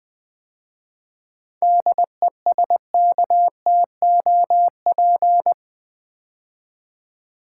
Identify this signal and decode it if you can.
Morse code, heard as a keyed tone, about 20 wpm, 700 Hz. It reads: DESKTOP
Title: 7 Letter Words - First 3 Letters Given - 20wpm